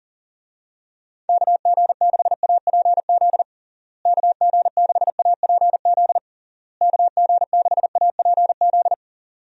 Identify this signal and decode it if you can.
Morse code, heard as a keyed tone, about 40 wpm, 700 Hz. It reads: KG6APZ KG6APZ KG6APZ
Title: USA Call Signs - 6 Characters - Repeated 3x - 40wpm